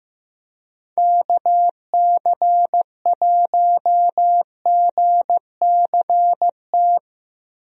Morse code, heard as a keyed tone, about 15 wpm, 700 Hz. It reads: KC1GCT
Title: USA Call Signs - 6 Characters - Repeated 3x - 15wpm